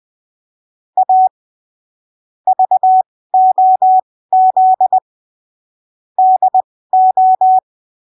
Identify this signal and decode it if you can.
Morse code, heard as a keyed tone, about 20 wpm, spaced as for 15 wpm, 750 Hz. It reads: A VOZ DO
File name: arquivos / texto / texto 13 20ppm